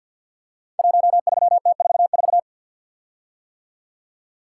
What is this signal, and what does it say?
Morse code, heard as a keyed tone, about 52 wpm, 700 Hz. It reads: 13T44
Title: Code Groups - 5 Letter:Numbers - 50wpm